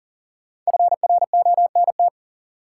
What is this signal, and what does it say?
Morse code, heard as a keyed tone, about 37 wpm, 700 Hz. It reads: FRONT
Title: Sentences from Top 500 Words - New Word Drill - 40wpm